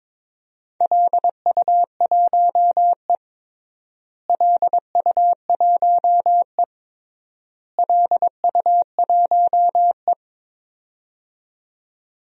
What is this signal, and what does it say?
Morse code, heard as a keyed tone, about 21 wpm, 700 Hz. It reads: LU1E LU1E LU1E
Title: DXCC-100 Call Signs - 4 Characters - Repeated 3x - 22wpm